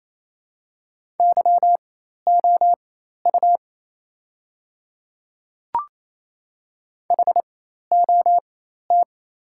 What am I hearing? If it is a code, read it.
Morse code, heard as a keyed tone, about 28 wpm, spaced as for 13 wpm, 700 Hz. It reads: YOU HOT
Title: 4x Character Spacing - Top 100 Words - 28wpm